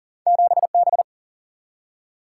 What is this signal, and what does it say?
Morse code, heard as a keyed tone, about 40 wpm, 700 Hz. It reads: 7B
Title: Single-Letter Single-Number 40wpm